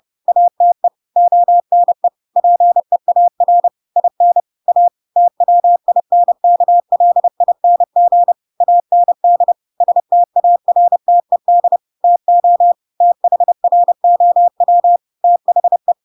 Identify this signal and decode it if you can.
Morse code, heard as a keyed tone, about 30 wpm, 700 Hz. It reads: ATE ONE PEAR IN A TWINKLING AND STARTED TO THROW THE
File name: Pinocchio / Pinocchio ch7 30 30 ik1hge